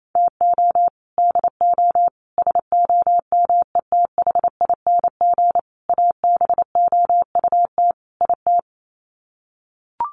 Morse code, heard as a keyed tone, about 28 wpm, 700 Hz. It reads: TO DO SOMETHING ABOUT IT
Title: Sets of 5 Words 28wpm